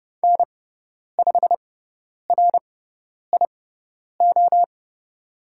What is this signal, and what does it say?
Morse code, heard as a keyed tone, about 29 wpm, spaced as for 10 wpm, 700 Hz. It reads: N5RIO
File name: Lesson 18g - Letter m - Call Signs 30wpm